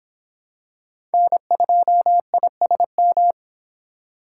Morse code, heard as a keyed tone, about 26 wpm, 700 Hz. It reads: N2ISM